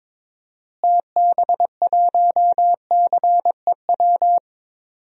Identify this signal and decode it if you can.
Morse code, heard as a keyed tone, about 22 wpm, 700 Hz. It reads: TB1CEW
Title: DXCC-100 Call Signs - 6 Characters - Repeated 2x - 22wpm